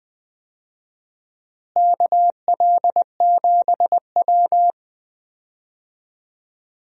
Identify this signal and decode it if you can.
Morse code, heard as a keyed tone, about 20 wpm, 700 Hz. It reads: KL7W
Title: DXCC-100 Call Signs - 4 Characters - Repeated 3x - 20wpm